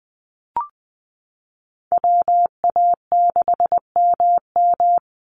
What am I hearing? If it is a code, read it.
Morse code, heard as a keyed tone, about 20 wpm, 700 Hz. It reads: WA6MM
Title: NA SOTA Chaser Callsigns 20wpm